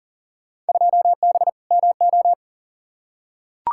Morse code, heard as a keyed tone, about 40 wpm, 700 Hz. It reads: 2B MO
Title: Field Day Exchanges Encore - 40wpm